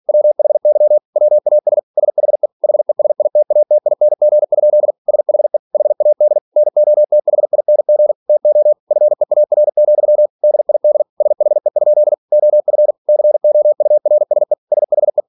Morse code, heard as a keyed tone, about 46 wpm, 600 Hz. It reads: WHY WAS SHE HESITATING? SHE HAD NOTHING TO FEAR, DID SHE? OF COURSE SHE
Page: Never 47wpm